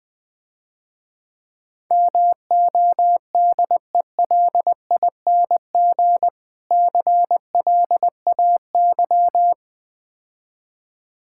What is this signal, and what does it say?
Morse code, heard as a keyed tone, about 20 wpm, 700 Hz. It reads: MODELING CLAY
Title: Difficult Words - Speed-Racing with Hints - 20wpm